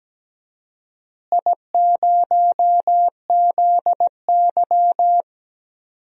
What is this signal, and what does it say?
Morse code, heard as a keyed tone, about 17 wpm, 700 Hz. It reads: I0ZY
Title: DXCC-100 Call Signs - 4 Characters - Repeated 3x - 17wpm